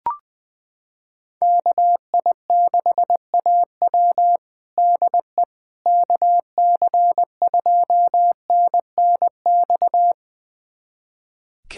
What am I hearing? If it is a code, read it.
Morse code, heard as a keyed tone, about 20 wpm, 700 Hz. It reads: KI6AW DE KC2NNX
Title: USA Callsign DE Callsign 20wpm